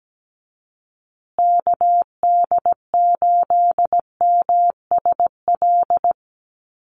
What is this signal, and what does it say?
Morse code, heard as a keyed tone, about 17 wpm, 700 Hz. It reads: KD8MSL
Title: USA Call Signs Repeated 3x 17wpm